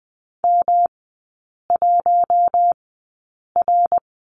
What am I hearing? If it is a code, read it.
Morse code, heard as a keyed tone, about 20 wpm, spaced as for 8 wpm, 700 Hz. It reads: M1R
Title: 178 Lesson 29e - Slash - Three Characters 20wpm